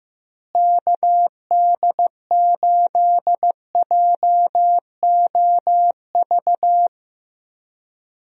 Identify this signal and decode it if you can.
Morse code, heard as a keyed tone, about 16 wpm, 700 Hz. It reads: KD8JOV
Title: USA Call Signs - 6 Characters - Repeated 3x - 15wpm